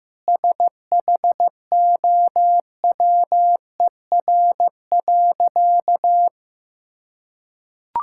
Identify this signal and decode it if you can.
Morse code, heard as a keyed tone, about 15 wpm, 700 Hz. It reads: SHOWER.
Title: Sentences from Top 1600 Words 15wpm